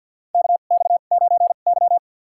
Morse code, heard as a keyed tone, about 52 wpm, spaced as for 36 wpm, 700 Hz. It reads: KX9Y
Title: USA Call Signs - 4 Characters - 2x Character Spacing - 50wpm